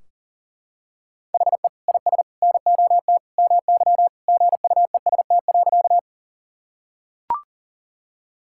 Morse code, heard as a keyed tone, about 40 wpm, 700 Hz. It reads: HE IS NOT MY GUEST.
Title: Sentences from Top 1600 Words 40wpm